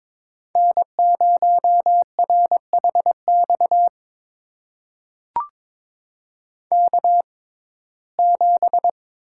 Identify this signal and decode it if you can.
Morse code, heard as a keyed tone, about 22 wpm, spaced as for 16 wpm, 700 Hz. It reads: N0RHX K7
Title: USA Call Signs - 5 Characters - 6x Character Spacing - 22wpm